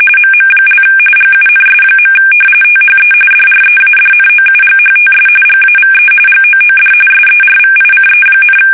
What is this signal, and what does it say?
RTTY, about 45.45 baud. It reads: MM) GUNS. THE DREADNOUGHTS GAVE GOOD SERVICE,
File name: fsk850 / battleships